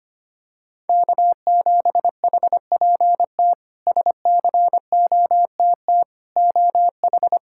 Morse code, heard as a keyed tone, about 25 wpm, 700 Hz. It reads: K7HPT SCOTT OH
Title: K1USN SST QSO Practice - 25wpm